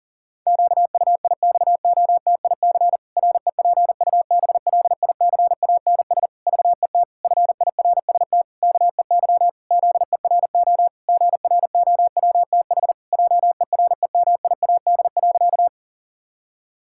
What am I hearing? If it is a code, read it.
Morse code, heard as a keyed tone, about 40 wpm, 700 Hz. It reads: QUIXOTIC REPUBLICANS VET FIRST KEY ZERO GROWTH JEREMIAD.
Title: Pangram Challenge v2 - 40wpm